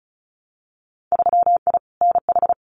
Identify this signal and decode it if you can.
Morse code, heard as a keyed tone, about 34 wpm, 700 Hz. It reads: 3I NH
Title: Winter Field Day Exchanges Encore 35wpm